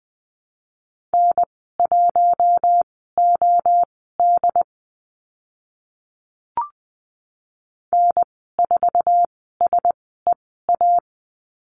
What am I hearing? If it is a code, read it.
Morse code, heard as a keyed tone, about 20 wpm, spaced as for 15 wpm, 700 Hz. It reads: N1OD N4SEA
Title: 076 Lesson 13g - Letter L - Call Signs 20wpm